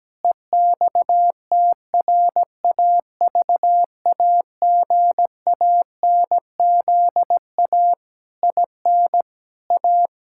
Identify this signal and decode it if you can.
Morse code, heard as a keyed tone, about 17 wpm, 700 Hz. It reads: EXTRAVAGANZA IN A
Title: Pangram Challenge v2 - 17wpm